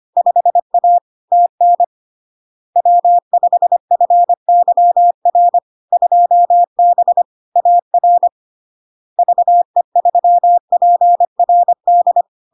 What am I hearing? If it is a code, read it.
Morse code, heard as a keyed tone, about 25 wpm, 700 Hz. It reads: 5A TN W5FYR 2B AR VE3PRD
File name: LARK-FD-2022-Log-in-CW-25wpm-700-Hz